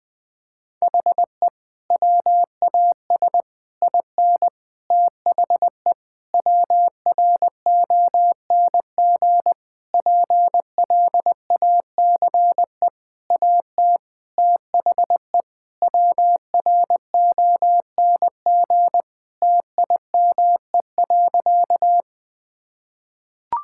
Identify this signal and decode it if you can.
Morse code, heard as a keyed tone, about 20 wpm, 700 Hz. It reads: HE WAS IN THE WRONG PLACE AT THE WRONG TIME.
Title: Sentences with 8-Word Phrases - 20wpm